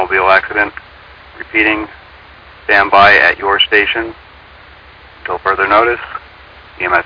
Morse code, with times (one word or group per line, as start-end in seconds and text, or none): none